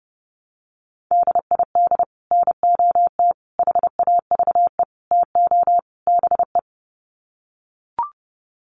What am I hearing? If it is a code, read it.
Morse code, heard as a keyed tone, about 28 wpm, 700 Hz. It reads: DID NOT HAVE TO BE
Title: Sets of 5 Words Encore2 30wpm